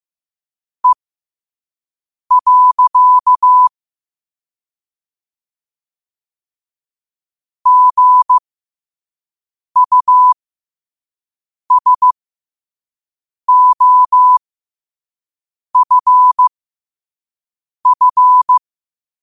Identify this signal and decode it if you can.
Morse code, heard as a keyed tone, about 15 wpm, spaced as for 5 wpm, 1000 Hz. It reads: E. GUSOFF
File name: qotdfiles / 5wpm